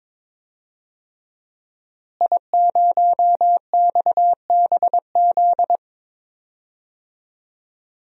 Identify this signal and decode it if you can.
Morse code, heard as a keyed tone, about 22 wpm, 700 Hz. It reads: I0XBZ
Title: DXCC-100 Call Signs - 5 Characters - Repeated 2x - 22wpm